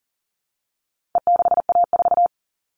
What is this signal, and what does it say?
Morse code, heard as a keyed tone, about 40 wpm, 700 Hz. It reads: E6A4
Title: All Cut Numbers - 4-Digit 40wpm